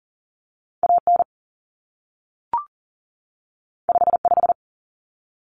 Morse code, heard as a keyed tone, about 40 wpm, 700 Hz. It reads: AN 55
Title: Common Cut Numbers - 2-Digit 40wpm